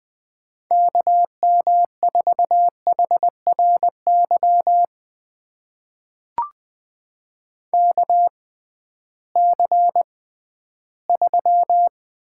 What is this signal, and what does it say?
Morse code, heard as a keyed tone, about 20 wpm, spaced as for 13 wpm, 700 Hz. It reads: KM4HRY KC3
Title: USA Call Signs - 6 Characters - 6x Character Spacing - 20wpm